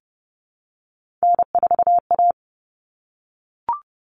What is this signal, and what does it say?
Morse code, heard as a keyed tone, about 30 wpm, 700 Hz. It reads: N4A
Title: All Cut Numbers - 3-Digit 30wpm